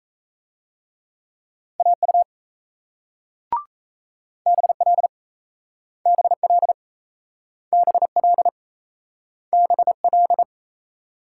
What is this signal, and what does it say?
Morse code, heard as a keyed tone, about 38 wpm, 700 Hz. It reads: AU BL BL BL BL